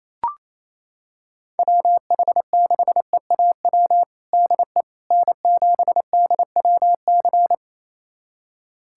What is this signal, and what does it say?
Morse code, heard as a keyed tone, about 28 wpm, 700 Hz. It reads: WH6EAW DE N7DWC
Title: USA Callsign DE Callsign 28wpm